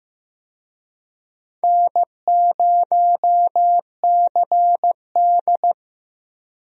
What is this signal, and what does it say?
Morse code, heard as a keyed tone, about 15 wpm, 700 Hz. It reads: N0CD
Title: USA Call Signs - 4 Characters - 15wpm